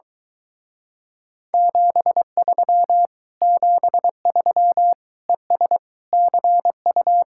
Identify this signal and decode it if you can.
Morse code, heard as a keyed tone, about 23 wpm, 700 Hz. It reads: 73 73 ES CU